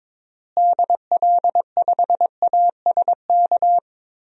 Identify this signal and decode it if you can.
Morse code, heard as a keyed tone, about 22 wpm, 700 Hz. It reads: DL5ASK
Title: DXCC-100 Call Signs - 6 Characters - Repeated 3x - 22wpm